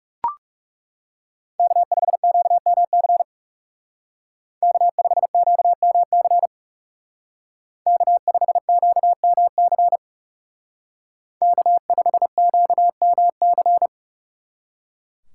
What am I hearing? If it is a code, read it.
Morse code, heard as a keyed tone, about 36 wpm, 700 Hz. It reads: K5QMC K5QMC K5QMC K5QMC
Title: Speed Racing - USA 1x3 Call Signs 30wpm